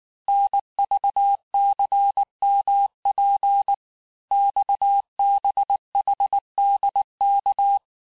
Morse code, heard as a keyed tone, about 19 wpm, 800 Hz. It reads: NVCMP XBHDK
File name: cours-du-lundi files / cours-1583--texte-chiffre-18-mpm